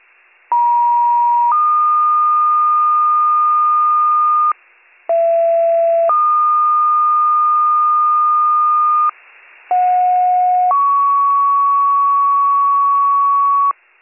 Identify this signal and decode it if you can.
Two-tone page, heard as a one-second tone, then a three-second tone, about 950 then 1220 Hz, about 670 then 1220 Hz, about 730 then 1120 Hz. It is heard from 0.5 to 13.7 s.